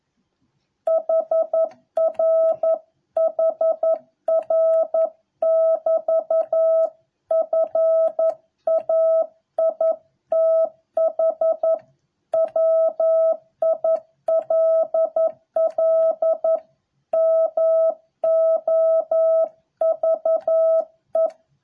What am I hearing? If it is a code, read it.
Morse code, heard as a keyed tone, about 11 wpm, 650 Hz. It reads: HR HR <BT> FAITH WILL MOVE